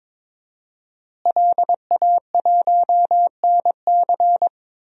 Morse code, heard as a keyed tone, about 22 wpm, 700 Hz. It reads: LA1NC
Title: DXCC-100 Call Signs - 5 Characters - Repeated 2x - 22wpm